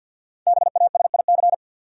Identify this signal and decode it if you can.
Morse code, heard as a keyed tone, about 52 wpm, 700 Hz. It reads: BASIC
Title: Sentences from Top 1000 Words - Review Part 3 - New Word Drill - 50wpm